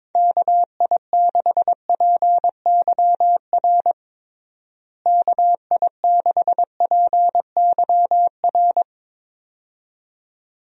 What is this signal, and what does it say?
Morse code, heard as a keyed tone, about 22 wpm, 700 Hz. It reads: KI6PYR KI6PYR
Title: USA Call Signs - 6 Characters - Repeated 2x - 22wpm